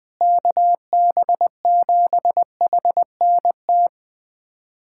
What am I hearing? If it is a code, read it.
Morse code, heard as a keyed tone, about 21 wpm, 700 Hz. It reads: KB7HNT
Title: USA Callsign DE Callsign - 6 Characters - 20wpm